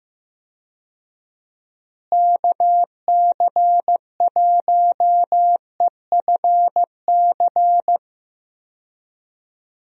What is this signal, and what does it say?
Morse code, heard as a keyed tone, about 15 wpm, 700 Hz. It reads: KC1EFC
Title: USA Call Signs - 6 Characters - Repeated 3x - 15wpm